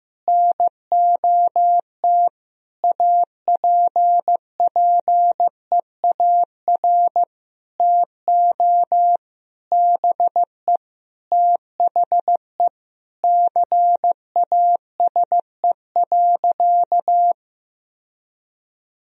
Morse code, heard as a keyed tone, about 15 wpm, 700 Hz. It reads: NOT APPEAR TO BE THE CASE.
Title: Sentences from Top 1000 Words - Review Part 2 - 15wpm